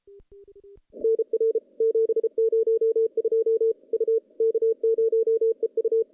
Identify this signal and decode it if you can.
Morse code, heard as a keyed tone, about 33 wpm, 410 Hz. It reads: NR 702 U K0EU
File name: contests / CW-SS-k5tr- -0400